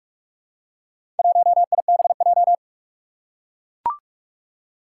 Morse code, heard as a keyed tone, about 44 wpm, 700 Hz. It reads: 1IBJ
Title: Four Challenging Letter-Number Combinations 45wpm